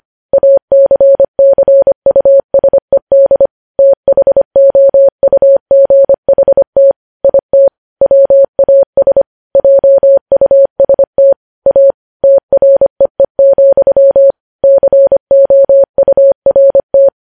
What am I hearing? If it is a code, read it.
Morse code, heard as a keyed tone, about 25 wpm, 550 Hz. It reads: ACCUSED THOUGHT IT WAS JUST A TREE, COURT
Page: news headlines in morse code at 25 wpm